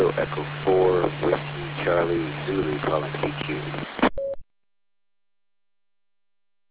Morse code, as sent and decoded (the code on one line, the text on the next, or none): none